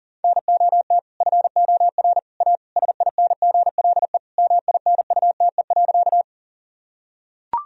Morse code, heard as a keyed tone, about 37 wpm, 700 Hz. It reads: NOT FOR A SINGLE MINUTE.
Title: Sentences from Top 1000 Words - Review Part 4 - 40wpm